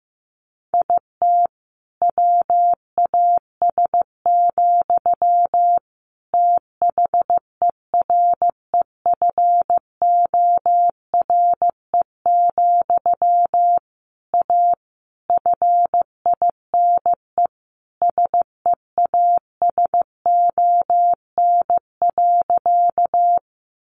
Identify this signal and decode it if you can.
Morse code, heard as a keyed tone, about 14 wpm, 700 Hz. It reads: IT WAS, THEREFORE, A FINE SEASON.
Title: Sentences from Top 1400 Words 15wpm